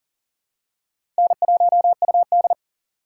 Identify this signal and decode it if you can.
Morse code, heard as a keyed tone, about 40 wpm, 700 Hz. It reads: N1UD
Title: USA Call Signs - 4 Characters - 3x Character Spacing - 40wpm